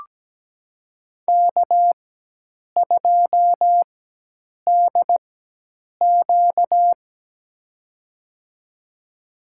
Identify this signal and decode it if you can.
Morse code, heard as a keyed tone, about 17 wpm, spaced as for 8 wpm, 700 Hz. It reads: K2DQ
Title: USA Call Signs - 4 Characters - 4x Character Spacing - 17wpm